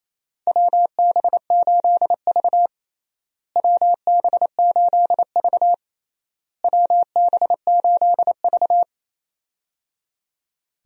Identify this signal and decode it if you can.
Morse code, heard as a keyed tone, about 28 wpm, 700 Hz. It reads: WB8V WB8V WB8V
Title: USA Call Signs - 4 Characters - Repeated 3x - 28wpm